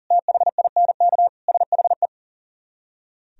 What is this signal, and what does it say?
Morse code, heard as a keyed tone, about 51 wpm, 700 Hz. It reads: THINK SHE